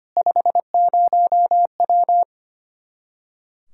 Morse code, heard as a keyed tone, about 25 wpm, 700 Hz. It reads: 50W
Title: Practice QSO - Exchange 2 - Info - 25wpm